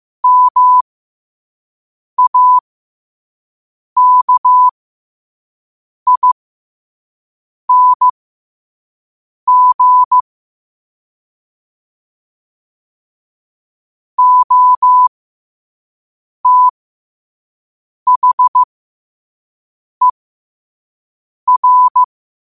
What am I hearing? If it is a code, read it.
Morse code, heard as a keyed tone, about 15 wpm, spaced as for 5 wpm, 1000 Hz. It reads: MAKING OTHER
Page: New quotes every day in morse code at 5 Words per minute.